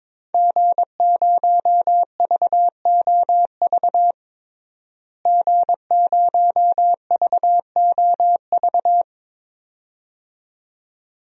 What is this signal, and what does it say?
Morse code, heard as a keyed tone, about 22 wpm, 700 Hz. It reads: G0VOV G0VOV
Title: DXCC-100 Call Signs - 5 Characters - Repeated 2x - 22wpm